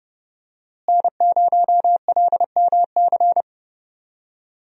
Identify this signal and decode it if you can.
Morse code, heard as a keyed tone, about 30 wpm, 700 Hz. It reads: N0LMC
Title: USA Call Signs - 5 Characters - Repeated 3x - 30wpm